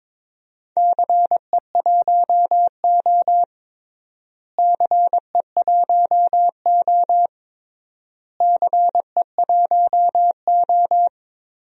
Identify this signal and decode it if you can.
Morse code, heard as a keyed tone, about 22 wpm, 700 Hz. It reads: CE1O CE1O CE1O
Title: DXCC-100 Call Signs - 4 Characters - Repeated 3x - 22wpm